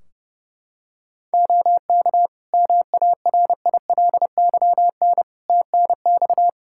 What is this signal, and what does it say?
Morse code, heard as a keyed tone, about 30 wpm, 700 Hz. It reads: OK MARILYN TNX
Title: Practice QSO - Exchange 4 - Ending - 30wpm